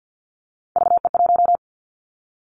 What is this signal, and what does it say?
Morse code, heard as a keyed tone, about 50 wpm, 700 Hz. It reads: VE1
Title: Callsign Prefixes for Top 100 DXCC - 50wpm